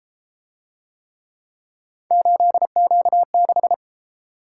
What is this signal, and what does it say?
Morse code, heard as a keyed tone, about 31 wpm, 700 Hz. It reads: 8Q6